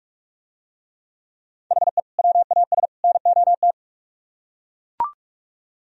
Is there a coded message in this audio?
Yes: Morse code from 1.7 to 3.7 s, heard as a keyed tone, about 42 wpm, 700 Hz. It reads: HE WAS NOT